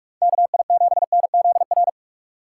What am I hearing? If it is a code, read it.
Morse code, heard as a keyed tone, about 46 wpm, 700 Hz. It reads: KI7NZR